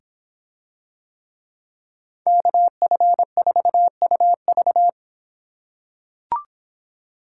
Morse code, heard as a keyed tone, about 26 wpm, 700 Hz. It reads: KF4UV